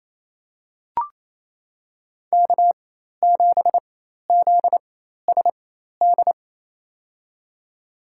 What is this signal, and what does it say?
Morse code, heard as a keyed tone, about 28 wpm, spaced as for 13 wpm, 700 Hz. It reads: K7ZSD
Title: USA Call Signs - 5 Characters - 4x Character Spacing - 28wpm